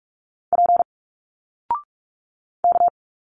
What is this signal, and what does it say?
Morse code, heard as a keyed tone, about 43 wpm, 700 Hz. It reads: P K
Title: Single Letters 45wpm